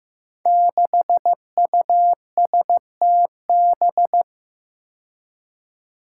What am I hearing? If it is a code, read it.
Morse code, heard as a keyed tone, about 15 wpm, 700 Hz. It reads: 6USTB
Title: Five Challenging Letter-Number Combinations 15wpm